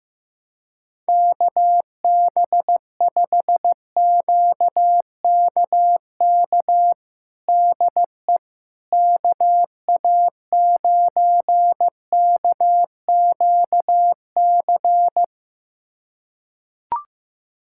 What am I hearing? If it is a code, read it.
Morse code, heard as a keyed tone, about 16 wpm, 700 Hz. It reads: KB5QKK DE KA9KQC
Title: USA Callsign DE Callsign - 6 Characters - 15wpm